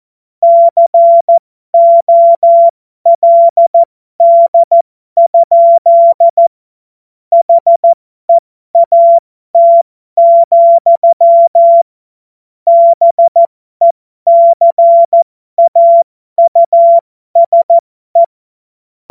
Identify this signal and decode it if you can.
Morse code, heard as a keyed tone, about 14 wpm, spaced as for 12 wpm, 680 Hz. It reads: COLD? HEAT, BECAUSE
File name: Morsekurs / Witze